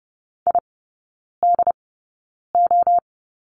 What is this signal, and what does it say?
Morse code, heard as a keyed tone, about 30 wpm, spaced as for 9 wpm, 700 Hz. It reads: IDO
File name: course / 30:10wpm / 087 Lesson 15e - Number 5 - Three Characters 30wpm